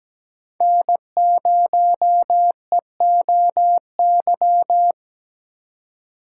Morse code, heard as a keyed tone, about 17 wpm, 700 Hz. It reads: N0EOY
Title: USA Call Signs - 5 Characters - 4x Character Spacing - 17wpm